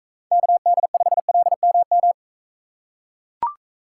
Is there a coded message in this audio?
Yes: Morse code from 0.3 to 2.1 s, heard as a keyed tone, about 43 wpm, 700 Hz. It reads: KD5LMM